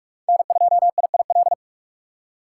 Morse code, heard as a keyed tone, about 44 wpm, 700 Hz. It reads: N2IIL
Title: USA Call Signs - 3x Character Spacing - 45wpm